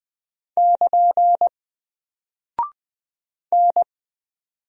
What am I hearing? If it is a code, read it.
Morse code, heard as a keyed tone, about 20 wpm, 700 Hz. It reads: <KN> N
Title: Single Character (Letter:Number:Punctuation:Prosign) 20wpm